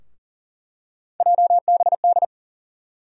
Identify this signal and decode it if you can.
Morse code, heard as a keyed tone, about 39 wpm, 700 Hz. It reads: JBD
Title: USA Call Sign Postfixes - 3 Letters 40wpm